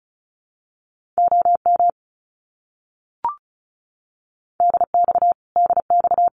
Morse code, heard as a keyed tone, about 34 wpm, 700 Hz. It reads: OM DX DX
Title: Abbreviations Repeated 3x - 35wpm